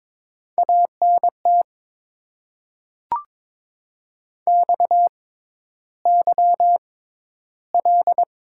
Morse code, heard as a keyed tone, about 22 wpm, spaced as for 11 wpm, 700 Hz. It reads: ANT XYL
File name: Top 90 QSO Elements - 6x Character Spacing - 22wpm